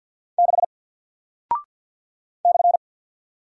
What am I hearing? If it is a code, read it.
Morse code, heard as a keyed tone, about 48 wpm, 700 Hz. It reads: 6 /
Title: Single Character (Letter:Number:Punctuation:Prosign) 50wpm